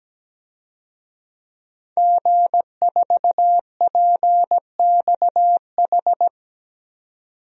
Morse code, heard as a keyed tone, about 17 wpm, 700 Hz. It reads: G4PXH
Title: DXCC-100 Call Signs - 5 Characters - 17wpm